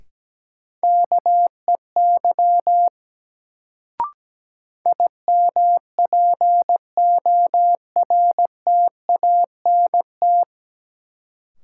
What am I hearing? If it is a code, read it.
Morse code, heard as a keyed tone, about 17 wpm, 700 Hz. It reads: KEY IMPORTANT
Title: Sentences from Top 600 Words - New Word Drill - 17wpm